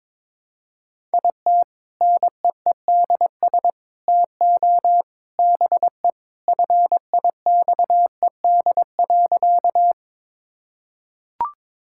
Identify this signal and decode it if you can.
Morse code, heard as a keyed tone, about 21 wpm, 700 Hz. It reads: IT NEEDS TO BE FIXED.
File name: practice / 22wpm / Sentences with 3-Word Phrases - 22wpm